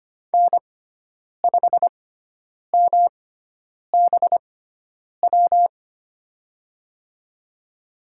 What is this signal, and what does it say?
Morse code, heard as a keyed tone, about 25 wpm, spaced as for 9 wpm, 700 Hz. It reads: N5MBW
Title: USA Call Signs - 5 Characters - 6x Character Spacing - 25wpm